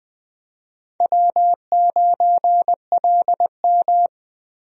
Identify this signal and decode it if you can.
Morse code, heard as a keyed tone, about 20 wpm, 700 Hz. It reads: W9LM
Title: USA Call Signs - 4 Characters - Repeated 2x - 20wpm